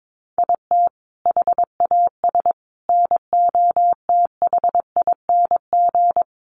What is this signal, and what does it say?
Morse code, heard as a keyed tone, about 22 wpm, 700 Hz. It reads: IT HAS NOTHING
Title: Sets of 5 Words Encore 22wpm